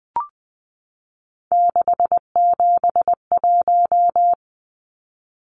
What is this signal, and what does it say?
Morse code, heard as a keyed tone, about 20 wpm, 700 Hz. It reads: 671
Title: Numbers - 3-Digit 20wpm